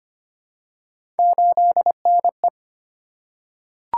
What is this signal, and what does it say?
Morse code, heard as a keyed tone, about 25 wpm, 700 Hz. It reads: 8NE
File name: Three Letter-Number Combinations - 25wpm